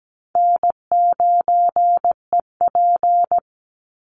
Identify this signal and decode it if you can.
Morse code, heard as a keyed tone, about 17 wpm, 700 Hz. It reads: N9EP
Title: Easy Call Signs Repeated 2x - 17wpm